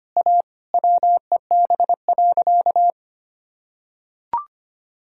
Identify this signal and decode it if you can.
Morse code, heard as a keyed tone, about 25 wpm, 700 Hz. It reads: A WEB.
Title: Sentences from Top 1000 Words - Review Part 4 - 25wpm